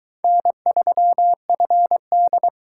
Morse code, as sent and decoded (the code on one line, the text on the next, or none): -. ...-- ..-. -..
N3FD